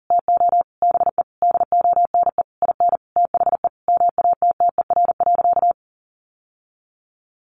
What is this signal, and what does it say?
Morse code, heard as a keyed tone, about 40 wpm, 700 Hz. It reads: TO BE DONE IN THE MATTER.
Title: Sentences from Top 700 Words 40wpm